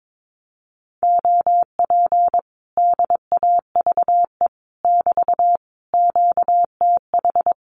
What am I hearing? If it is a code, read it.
Morse code, heard as a keyed tone, about 22 wpm, 700 Hz. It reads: OP DAVE <BT> QTH
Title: Common QSO Phrases Encore 22wpm